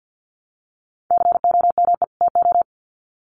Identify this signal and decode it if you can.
Morse code, heard as a keyed tone, about 50 wpm, 700 Hz. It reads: COME TO
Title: Sets of 2 Words 50wpm